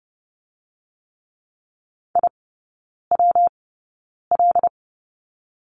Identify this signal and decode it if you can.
Morse code, heard as a keyed tone, about 30 wpm, 700 Hz. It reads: I W L
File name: Lesson 19e - Letter w - Three Characters 30wpm